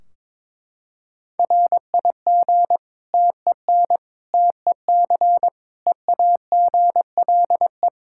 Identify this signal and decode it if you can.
Morse code, heard as a keyed tone, about 21 wpm, 700 Hz. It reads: RIG TEN TEC EAGLE
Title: Practice QSO - Exchange 2 - Info - 22wpm